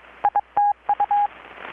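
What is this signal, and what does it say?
Morse code, heard as a keyed tone, about 22 wpm, 820 Hz, with radio noise. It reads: ITU
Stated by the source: Morse from KPH's Night of Nights XX maritime radiotelegraphy event, received in Chiapas, Mexico about 3,800 km away on a Kenwood R-600 with a grounded 10 m long-wire antenna.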